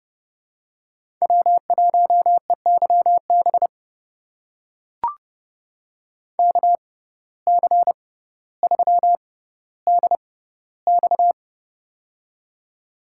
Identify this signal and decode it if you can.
Morse code, heard as a keyed tone, about 30 wpm, spaced as for 15 wpm, 700 Hz. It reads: W1EYB KC3DX
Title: USA Call Signs - 5 Characters - 6x Character Spacing - 30wpm